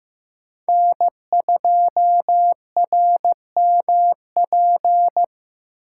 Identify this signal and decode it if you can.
Morse code, heard as a keyed tone, about 15 wpm, 700 Hz. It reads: N2RMP
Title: USA Call Signs - 5 Characters - 6x Character Spacing - 15wpm